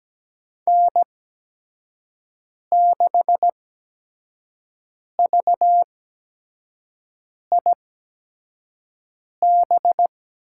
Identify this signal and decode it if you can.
Morse code, heard as a keyed tone, about 17 wpm, spaced as for 5 wpm, 700 Hz. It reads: N6VIB
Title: USA Call Signs - 5 Characters - 8x Character Spacing - 17wpm